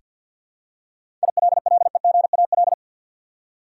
Morse code, heard as a keyed tone, about 49 wpm, 700 Hz. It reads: ILLEGAL
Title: Seven Letter Words - 50wpm